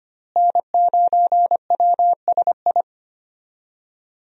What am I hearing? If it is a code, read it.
Morse code, heard as a keyed tone, about 25 wpm, 700 Hz. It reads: N9WSI
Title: USA Call Signs - 5 Characters - Repeated 3x - 25wpm